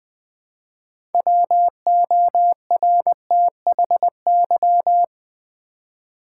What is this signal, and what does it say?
Morse code, heard as a keyed tone, about 20 wpm, 700 Hz. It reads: WORTHY
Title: 6 Letter Words - First 3 Letters Given - 20wpm